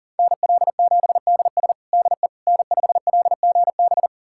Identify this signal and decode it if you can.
Morse code, heard as a keyed tone, about 40 wpm, 700 Hz. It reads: NL7DS DE N5LGB
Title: USA Callsign DE Callsign - 5 Characters - 40wpm